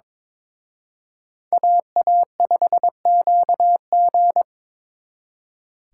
Morse code, heard as a keyed tone, about 26 wpm, 700 Hz. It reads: AA5QG
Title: Speed Racing - USA Call Signs 22wpm